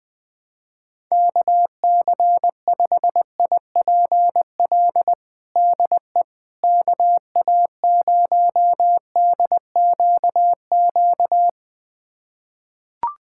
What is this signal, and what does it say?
Morse code, heard as a keyed tone, about 20 wpm, 700 Hz. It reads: KC5IPL DE KA0DQQ
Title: USA Callsign DE Callsign - 6 Characters - 20wpm